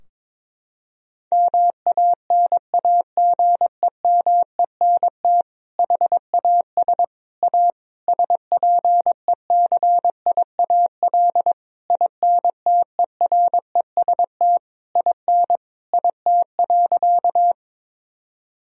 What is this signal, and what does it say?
Morse code, heard as a keyed tone, about 22 wpm, 700 Hz. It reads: MANAGEMENT HAS A SPECIAL INTEREST IN IT.
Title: Sentences from Top 1000 Words - Review Part 4 - 22wpm